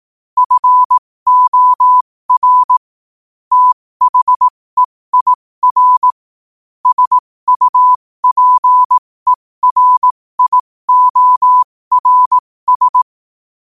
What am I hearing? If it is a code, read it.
Morse code, heard as a keyed tone, about 18 wpm, spaced as for 15 wpm, 1000 Hz. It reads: FOR THEIR SUPERIORS
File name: qotdfiles / 15wpm